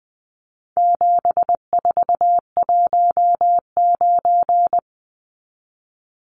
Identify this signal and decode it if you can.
Morse code, heard as a keyed tone, about 20 wpm, 700 Hz. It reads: 7419
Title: Numbers - 4-Digit 20wpm